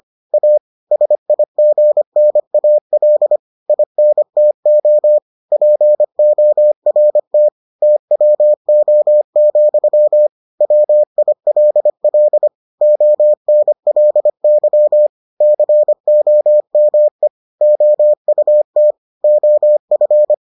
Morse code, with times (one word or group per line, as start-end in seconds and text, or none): 0.3-0.6 A
0.9-3.4 SIGNAL
3.7-5.2 INTO
5.5-7.5 PORT
7.8-10.3 TWO,
10.6-12.5 WILL
12.8-15.1 ONLY
15.4-17.3 COME
17.6-18.9 OUT
19.2-20.3 OF